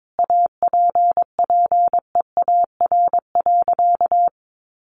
Morse code, heard as a keyed tone, about 22 wpm, 700 Hz. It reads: APPEAR.
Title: Sentences from Top 1200 Words 22wpm